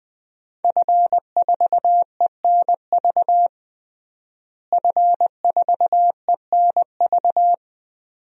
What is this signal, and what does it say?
Morse code, heard as a keyed tone, about 20 wpm, 700 Hz. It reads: F4ENV F4ENV
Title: DXCC-100 Call Signs - 5 Characters - Repeated 2x - 20wpm